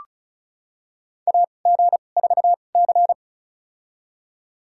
Morse code, heard as a keyed tone, about 36 wpm, spaced as for 25 wpm, 700 Hz. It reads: AG4C
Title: USA Call Signs - 4 Characters - 2x Character Spacing - 35wpm